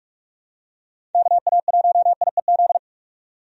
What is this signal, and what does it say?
Morse code, heard as a keyed tone, about 46 wpm, 700 Hz. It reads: KA1IEZ
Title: USA Call Signs - 6 Characters - 45wpm